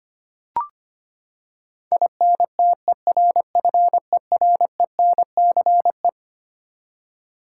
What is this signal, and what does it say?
Morse code, heard as a keyed tone, about 25 wpm, 700 Hz. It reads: INTERFERENCE
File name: Twelve Letter Words - 25wpm